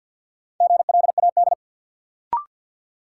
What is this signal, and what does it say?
Morse code, heard as a keyed tone, about 53 wpm, 700 Hz. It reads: GLAD
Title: Sentences from Top 800 Words - New Word Drill - 50wpm